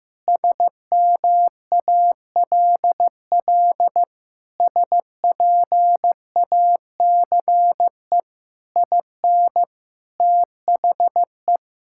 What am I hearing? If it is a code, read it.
Morse code, heard as a keyed tone, about 15 wpm, 700 Hz. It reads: SMALL SPACE IN THE
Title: Sentences from Top 1000 Words - Review Part 5 - 15wpm